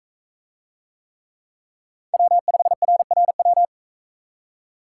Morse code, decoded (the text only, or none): W5RRW